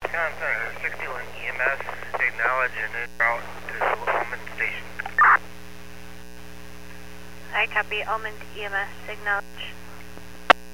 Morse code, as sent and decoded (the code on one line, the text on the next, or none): none